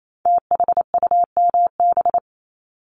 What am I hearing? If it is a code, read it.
Morse code, heard as a keyed tone, about 28 wpm, 700 Hz. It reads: THUMB
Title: Multi-letter Phonemes 28wpm